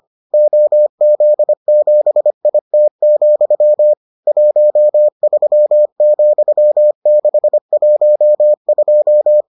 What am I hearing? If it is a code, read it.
Morse code, heard as a keyed tone, about 25 wpm, 600 Hz. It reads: OZ7IT, 13,612